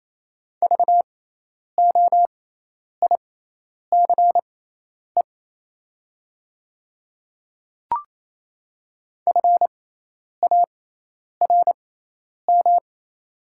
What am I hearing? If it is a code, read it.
Morse code, heard as a keyed tone, about 28 wpm, spaced as for 10 wpm, 700 Hz. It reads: VOICE FARM
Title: 6x Character Spacing - Top 500 Words - 28wpm